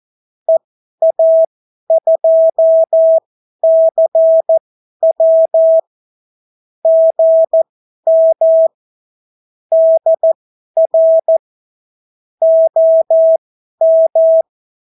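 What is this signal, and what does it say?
Morse code, heard as a keyed tone, about 14 wpm, spaced as for 11 wpm, 650 Hz. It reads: EA2CW GM DR OM
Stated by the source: Example Spanish CW QSO from a Morse learning book, 14 wpm characters with 8 wpm spacing.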